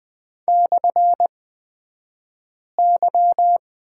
Morse code, heard as a keyed tone, about 20 wpm, spaced as for 5 wpm, 700 Hz. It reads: /Y
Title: 254 Lesson 41d - Period - Two Characters 20wpm